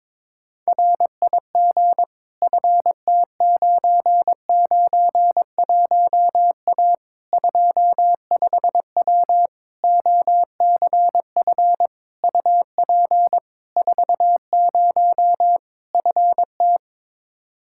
Morse code, heard as a keyed tone, about 22 wpm, 700 Hz. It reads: RIG FT991A 25W OCF UP 40 FT
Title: Practice QSO - Exchange 2 - Info - 22wpm